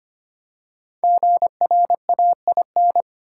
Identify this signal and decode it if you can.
Morse code, heard as a keyed tone, about 25 wpm, 700 Hz. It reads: GRAIN
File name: 5 Letter Words - First 3 Letters Given - 25wpm